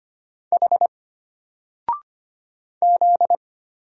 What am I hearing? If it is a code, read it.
Morse code, heard as a keyed tone, about 25 wpm, 700 Hz. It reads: H Z
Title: Single Letter-Number Alt 25wpm